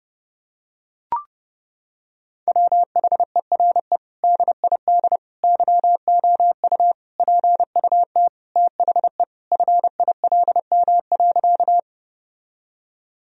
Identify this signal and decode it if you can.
Morse code, heard as a keyed tone, about 30 wpm, 700 Hz. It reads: WHERE DID YOU PUT THE FILM.
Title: Sentences from Top 1000 Words - Review Part 3 - 30wpm